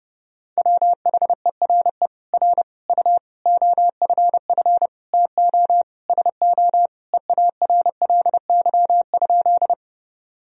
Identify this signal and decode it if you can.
Morse code, heard as a keyed tone, about 30 wpm, 700 Hz. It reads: WHERE R U OFF TO SO EARLY?
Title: Sentences from Top 400 Words and Abbreviations 30wpm